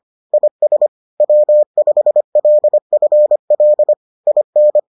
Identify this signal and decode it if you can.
Morse code, heard as a keyed tone, about 25 wpm, 600 Hz. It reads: IS W5LFL IN